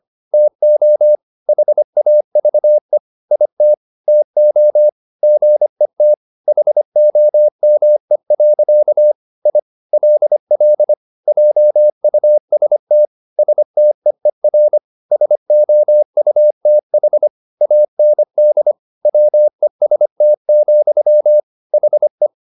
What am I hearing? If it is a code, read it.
Morse code, heard as a keyed tone, about 25 wpm, 600 Hz. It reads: TO HAVE IT TO GET HOME. I LL JUST STEER SOUTH AND WEST, HE